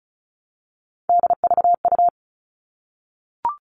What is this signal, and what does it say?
Morse code, heard as a keyed tone, about 34 wpm, 700 Hz. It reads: DVU
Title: All Cut Numbers - 3-Digit 35wpm